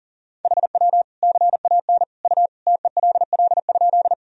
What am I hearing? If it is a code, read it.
Morse code, heard as a keyed tone, about 40 wpm, 700 Hz. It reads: HW CAN U TELL?
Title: Sentences from Top 200 Words and Abbreviations 40wpm